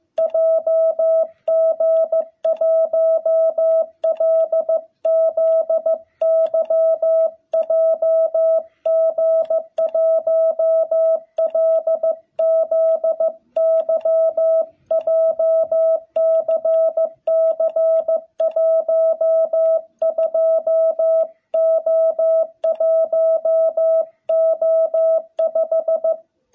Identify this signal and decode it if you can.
Morse code, heard as a keyed tone, about 15 wpm, 660 Hz. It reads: JG1LZY JG1LZY JCC 12O1O5
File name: jh8cbh / 2021cw